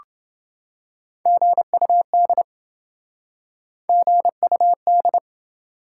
Morse code, heard as a keyed tone, about 28 wpm, 700 Hz. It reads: GUD GUD